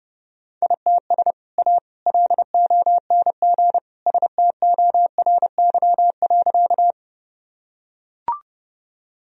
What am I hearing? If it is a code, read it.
Morse code, heard as a keyed tone, about 30 wpm, 700 Hz. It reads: ITS A LONG STORY.
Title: Sentences from Top 1000 Words - Review Part 1 - 30wpm